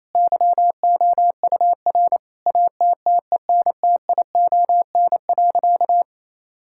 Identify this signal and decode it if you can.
Morse code, heard as a keyed tone, about 27 wpm, 690 Hz. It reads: YOUR ATTENTION.
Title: Sentences from Top 1000 Words - Review Part 6 - 28wpm